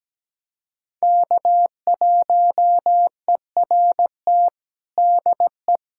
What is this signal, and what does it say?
Morse code, heard as a keyed tone, about 17 wpm, 700 Hz. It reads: K1ERT DE
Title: USA Callsign DE Callsign - 5 Characters - 17wpm